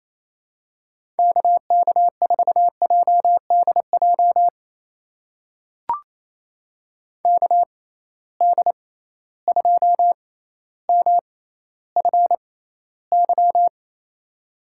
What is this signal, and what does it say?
Morse code, heard as a keyed tone, about 28 wpm, spaced as for 14 wpm, 700 Hz. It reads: KK4JDJ KD2MFY
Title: USA Call Signs - 6 Characters - 6x Character Spacing - 28wpm